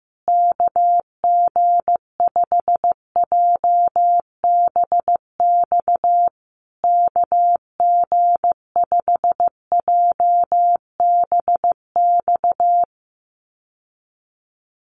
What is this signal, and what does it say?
Morse code, heard as a keyed tone, about 15 wpm, 700 Hz. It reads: KG5JBX KG5JBX
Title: USA Call Signs Repeated 3x 15wpm